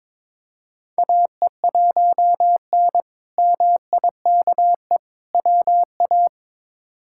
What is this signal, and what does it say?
Morse code, heard as a keyed tone, about 21 wpm, 700 Hz. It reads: AE1N MIKE WA
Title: K1USN SST QSO Practice - 22wpm